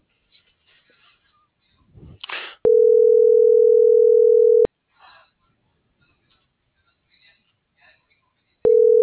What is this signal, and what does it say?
Ringback tone (North America), 440 + 480 Hz.
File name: CENTRAL NAVIERAS / out oscar.monta.o